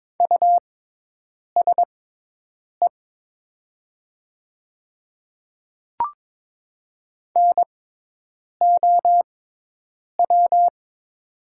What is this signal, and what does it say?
Morse code, heard as a keyed tone, about 21 wpm, spaced as for 8 wpm, 700 Hz. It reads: USE NOW
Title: 6x Character Spacing - Top 100 Words - 22wpm